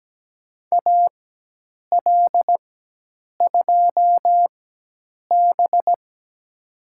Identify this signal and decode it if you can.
Morse code, heard as a keyed tone, about 17 wpm, spaced as for 8 wpm, 700 Hz. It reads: AL2B
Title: USA Call Signs - 4 Characters - 4x Character Spacing - 17wpm